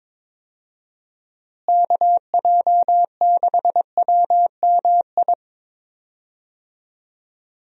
Morse code, heard as a keyed tone, about 22 wpm, 700 Hz. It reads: KJ6WMI
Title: USA Call Signs - 6 Characters - Repeated 3x - 22wpm